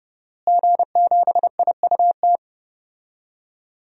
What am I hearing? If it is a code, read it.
Morse code, heard as a keyed tone, about 29 wpm, 700 Hz. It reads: G7IUT
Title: DXCC-100 Call Signs - 5 Characters - Repeated 2x - 30wpm